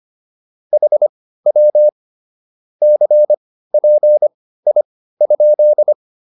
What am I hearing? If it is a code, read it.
Morse code, heard as a keyed tone, about 25 wpm, spaced as for 15 wpm, 600 Hz. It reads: HW CPI?
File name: QSO304 15